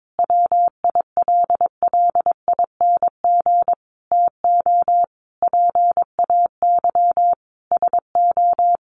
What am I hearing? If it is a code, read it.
Morse code, heard as a keyed tone, about 22 wpm, 700 Hz. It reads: WILLING TO PAY SO
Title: Sentences from Top 1400 Words 22wpm